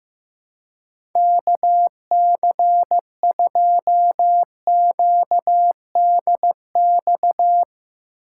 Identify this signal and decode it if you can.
Morse code, heard as a keyed tone, about 16 wpm, 700 Hz. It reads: KC2QDX
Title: USA Call Signs - 6 Characters - 6x Character Spacing - 15wpm